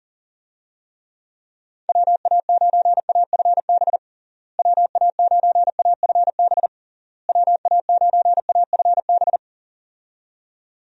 Morse code, heard as a keyed tone, about 40 wpm, 700 Hz. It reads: WA9AFB WA9AFB WA9AFB
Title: USA Call Signs - 6 Characters - Repeated 3x - 40wpm